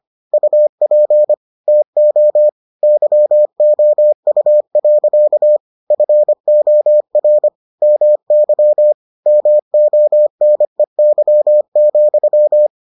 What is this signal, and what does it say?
Morse code, heard as a keyed tone, about 25 wpm, 600 Hz. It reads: UP TO YOU. FOR MY MONEY,